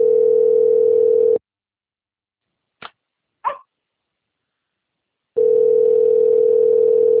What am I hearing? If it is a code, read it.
Ringback tone (North America), 440 + 480 Hz.
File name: out- - - - - .60125